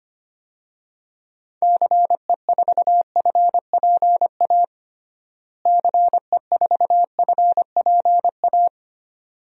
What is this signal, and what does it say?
Morse code, heard as a keyed tone, about 25 wpm, 700 Hz. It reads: CE4FPA CE4FPA
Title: DXCC-100 Call Signs - 6 Characters - Repeated 2x - 25wpm